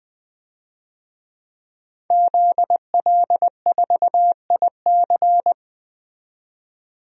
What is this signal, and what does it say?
Morse code, heard as a keyed tone, about 20 wpm, 700 Hz. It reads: ZL4IC
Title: DXCC-100 Call Signs - 5 Characters - 20wpm